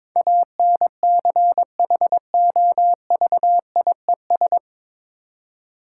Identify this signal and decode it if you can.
Morse code, heard as a keyed tone, about 21 wpm, 690 Hz. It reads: ANCHOVIES
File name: Difficult Words with Hints - 22wpm